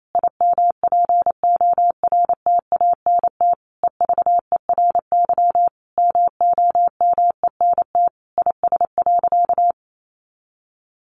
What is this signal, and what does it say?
Morse code, heard as a keyed tone, about 28 wpm, 700 Hz. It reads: IMPORTANT EVERY MOMENT IS.
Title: Sentences from Top 1000 Words 28wpm